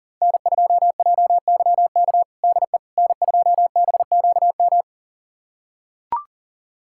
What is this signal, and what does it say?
Morse code, heard as a keyed tone, about 40 wpm, 700 Hz. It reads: N2JYK DE N2BQM
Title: USA Callsign DE Callsign - 5 Characters - 40wpm